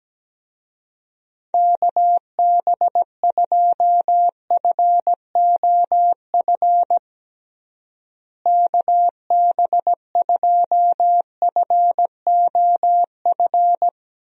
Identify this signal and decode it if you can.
Morse code, heard as a keyed tone, about 17 wpm, 700 Hz. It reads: KB2FOF KB2FOF
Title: USA Call Signs - 6 Characters - Repeated 2x - 17wpm